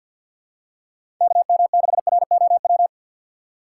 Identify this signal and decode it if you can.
Morse code, heard as a keyed tone, about 52 wpm, 700 Hz. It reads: KM6ROW
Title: USA Call Signs - 3x Character Spacing - 50wpm